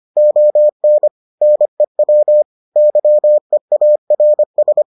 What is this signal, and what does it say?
Morse code, heard as a keyed tone, about 25 wpm, 600 Hz. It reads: ON NEW YEARS